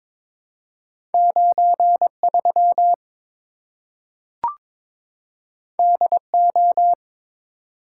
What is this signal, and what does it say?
Morse code, heard as a keyed tone, about 22 wpm, 700 Hz. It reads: 93 DO
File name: practice / 22wpm / Two Letter-Number Combinations - 22wpm